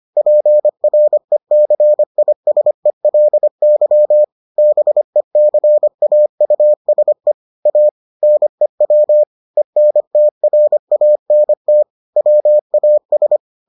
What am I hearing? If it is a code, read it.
Morse code, heard as a keyed tone, about 25 wpm, 600 Hz. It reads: PRECISELY BECAUSE A NEW ENTRANT WAS